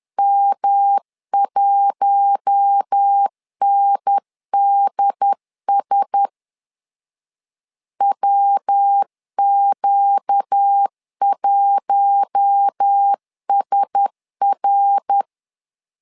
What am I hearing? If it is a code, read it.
Morse code, heard as a keyed tone, about 10 wpm, 800 Hz. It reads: M1NDS WQ1SR
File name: LEZIONI / 4 Q1S 10wpm 2133